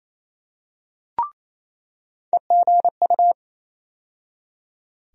Morse code, heard as a keyed tone, about 28 wpm, 700 Hz. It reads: EGU
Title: USA Call Sign Postfixes - 3 Letters 28wpm